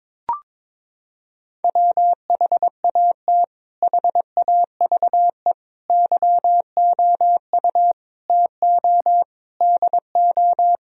Morse code, heard as a keyed tone, about 22 wpm, 700 Hz. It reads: WHAT HAVE YOU TO DO
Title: Sentences from Top 1000 Words - Review Part 2 - 22wpm